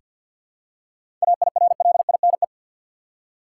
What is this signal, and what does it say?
Morse code, heard as a keyed tone, about 46 wpm, 700 Hz. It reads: AIRLINE
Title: Seven Letter Words - 50wpm